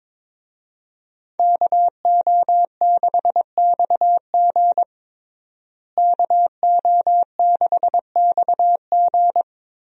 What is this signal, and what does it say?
Morse code, heard as a keyed tone, about 22 wpm, 700 Hz. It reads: KO6XG KO6XG
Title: USA Call Signs - 5 Characters - Repeated 3x - 22wpm